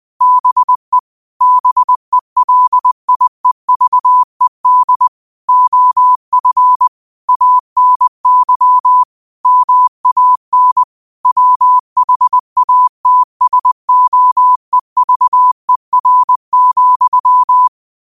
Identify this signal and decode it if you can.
Morse code, heard as a keyed tone, about 20 wpm, 1000 Hz. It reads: BE BELIEVED OF ANY MAN WHATSOEVER,
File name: qotdfiles / 20wpm